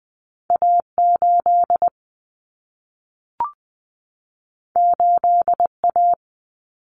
Morse code, heard as a keyed tone, about 20 wpm, 700 Hz. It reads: A8 8A
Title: Common Cut Numbers - 2-Digit 20wpm